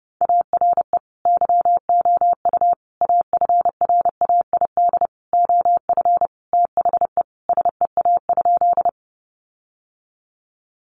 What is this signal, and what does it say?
Morse code, heard as a keyed tone, about 30 wpm, 700 Hz. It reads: ARE YOU AFRAID OF THE SEA?
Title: Sentences from Top 1500 Words 30wpm